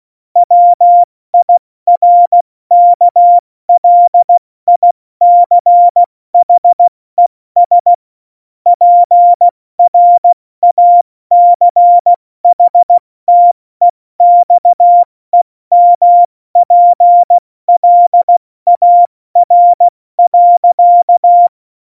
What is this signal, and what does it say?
Morse code, heard as a keyed tone, about 16 wpm, 700 Hz. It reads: WIRKLICHES PRACHTEXEMPLAR.
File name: Morsekurs / Witze / Witz 20. November 2018, 70 F80